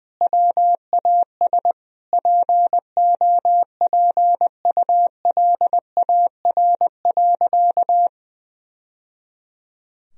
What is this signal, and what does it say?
Morse code, heard as a keyed tone, about 20 wpm, 700 Hz. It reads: WAS POPULAR.
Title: Sentences from Top 1000 Words - Review Part 3 - 20wpm